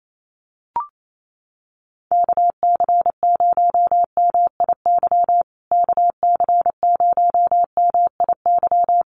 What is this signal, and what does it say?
Morse code, heard as a keyed tone, about 28 wpm, 700 Hz. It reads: KC0MIY KC0MIY
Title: USA Call Signs Repeated 3x 28wpm